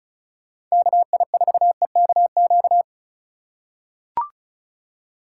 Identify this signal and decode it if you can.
Morse code, heard as a keyed tone, about 36 wpm, 700 Hz. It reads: KI4EKQ
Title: USA Call Signs - 4x Character Spacing - 35wpm